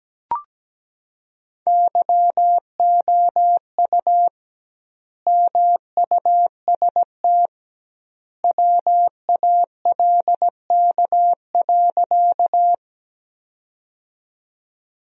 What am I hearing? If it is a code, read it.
Morse code, heard as a keyed tone, about 17 wpm, 700 Hz. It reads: YOU MUST WALK.
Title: 2x Word-Spacing - Sentences from Top 300 Words - 17wpm